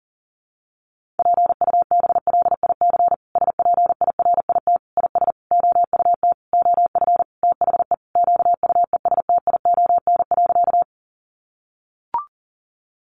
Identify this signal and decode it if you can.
Morse code, heard as a keyed tone, about 40 wpm, 700 Hz. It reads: PUBLIC SPIRIT IS OUT OF THE QUESTION.
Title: Sentences from Top 1400 Words 40wpm